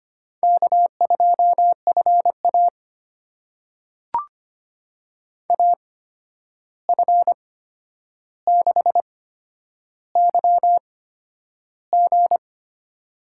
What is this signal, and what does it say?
Morse code, heard as a keyed tone, about 25 wpm, spaced as for 10 wpm, 700 Hz. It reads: K2FA AF6YG
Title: USA Call Signs - 8x Character Spacing - 25wpm